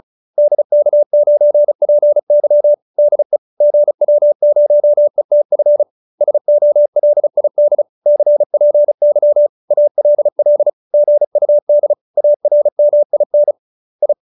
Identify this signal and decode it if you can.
Morse code, heard as a keyed tone, about 34 wpm, 590 Hz. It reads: DK9PY DE GW0ETF SOLID CPY ALL GUD ARMIN I